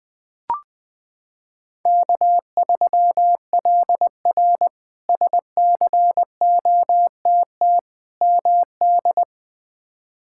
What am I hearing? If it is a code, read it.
Morse code, heard as a keyed tone, about 20 wpm, 700 Hz. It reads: K3LR SCOTT MD
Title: K1USN SST QSO Practice - 20wpm